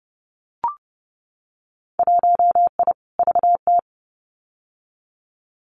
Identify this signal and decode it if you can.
Morse code, heard as a keyed tone, about 30 wpm, 700 Hz. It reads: 1I VT
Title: Winter Field Day Exchanges Encore 30wpm